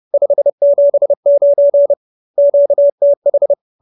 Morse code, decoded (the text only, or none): 579 QTH